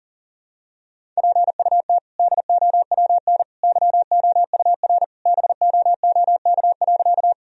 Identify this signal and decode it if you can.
Morse code, heard as a keyed tone, about 37 wpm, 700 Hz. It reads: PUT DOWN YOUR BOOK.
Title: Sentences from Top 1000 Words - Review Part 1 - 40wpm